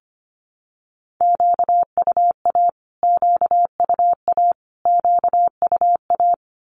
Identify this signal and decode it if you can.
Morse code, heard as a keyed tone, about 25 wpm, 700 Hz. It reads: QUA QUA QUA
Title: Q-Codes Repeated 3x 25wpm